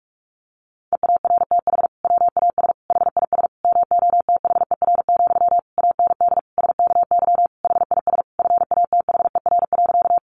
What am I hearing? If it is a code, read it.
Morse code, heard as a keyed tone, about 46 wpm, 700 Hz. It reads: EARTH WAS HIS MOTHER, AND SKY HIS FATHER.
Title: Sentences from Top 1200 Words 45wpm